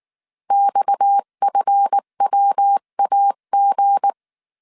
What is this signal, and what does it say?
Morse code, heard as a keyed tone, about 19 wpm, 800 Hz. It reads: XFWAG
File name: CW / DET / 1 9 1500C S B 18wpm 2551